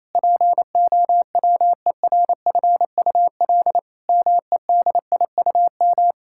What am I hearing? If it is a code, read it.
Morse code, heard as a keyed tone, about 28 wpm, 700 Hz. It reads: POWERFUL MEDIUM
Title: Sentences from Top 1000 Words - Review Part 6 - 28wpm